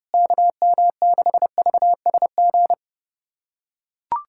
Morse code, heard as a keyed tone, about 32 wpm, 700 Hz. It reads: KM6VSG
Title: USA Callsign DE Callsign 30wpm